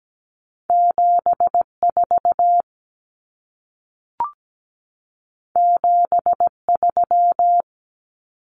Morse code, heard as a keyed tone, about 17 wpm, 700 Hz. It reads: 74 73
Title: Common Cut Numbers - 2-Digit 17wpm